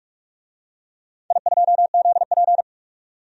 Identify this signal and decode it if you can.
Morse code, heard as a keyed tone, about 48 wpm, 700 Hz. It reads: I2ZP
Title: DXCC-100 Call Signs - 4 Characters - 45wpm